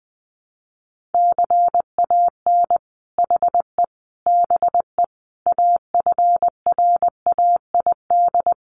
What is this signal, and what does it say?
Morse code, heard as a keyed tone, about 20 wpm, 700 Hz. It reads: CAN HE BE AFRAID
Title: Sentences from Top 1500 Words 20wpm